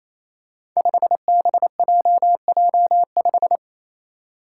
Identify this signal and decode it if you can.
Morse code, heard as a keyed tone, about 28 wpm, 700 Hz. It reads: 5BJJ5
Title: Five Challenging Letter-Number Combinations 28wpm